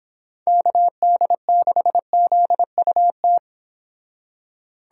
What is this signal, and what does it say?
Morse code, heard as a keyed tone, about 26 wpm, 700 Hz. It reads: KD6ZUT